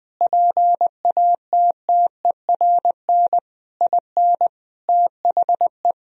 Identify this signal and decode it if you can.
Morse code, heard as a keyed tone, about 21 wpm, 700 Hz. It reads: PATTERN IN THE
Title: Sentences from Top 1000 Words - Review Part 2 - 20wpm